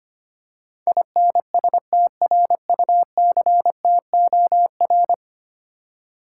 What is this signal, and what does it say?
Morse code, heard as a keyed tone, about 25 wpm, 700 Hz. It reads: INSTRUCTOR
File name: 10 Letter Words - First 3 Letters Given - 25wpm